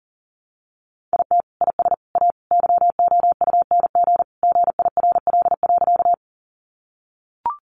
Morse code, heard as a keyed tone, about 40 wpm, 700 Hz. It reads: IT IS A YOUNG GIRL.
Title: Sentences from Top 400 Words 40wpm